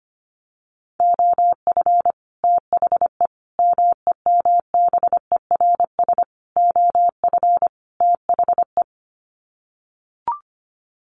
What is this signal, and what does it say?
Morse code, heard as a keyed tone, about 25 wpm, 700 Hz. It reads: OF THE MEMBERS OF THE
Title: Sets of 5 Words Encore2 25wpm